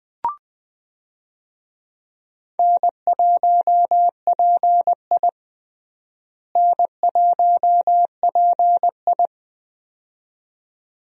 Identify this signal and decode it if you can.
Morse code, heard as a keyed tone, about 20 wpm, 700 Hz. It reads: N1PI N1PI
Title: USA Call Signs - 4 Characters - Repeated 2x - 20wpm